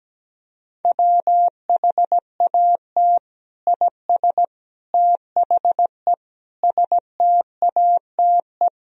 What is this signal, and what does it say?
Morse code, heard as a keyed tone, about 17 wpm, 700 Hz. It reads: WHAT IS THE STATE
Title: Sentences from Top 1000 Words - Review Part 3 - 17wpm